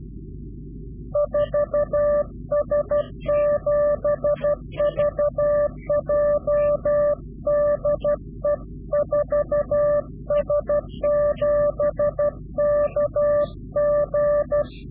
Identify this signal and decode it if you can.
Morse code, heard as a keyed tone, about 12 wpm, 590 Hz. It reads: 4S7VJ DE 4S7KG
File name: CW-QSO / Directional Call / Directional QSO